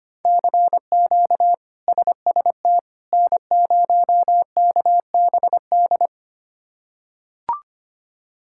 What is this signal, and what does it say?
Morse code, heard as a keyed tone, about 25 wpm, 700 Hz. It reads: CQ SST N0KBD
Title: K1USN SST QSO Practice - 25wpm